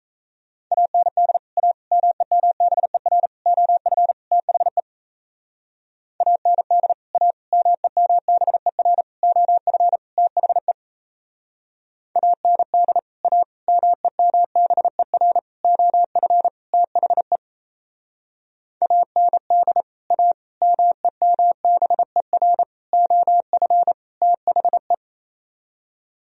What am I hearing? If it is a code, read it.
Morse code, heard as a keyed tone, about 35 wpm, 700 Hz. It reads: AND A MEMBER OF THE AND A MEMBER OF THE AND A MEMBER OF THE AND A MEMBER OF THE
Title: Speed-Racing - Sets of 5 Words 28wpm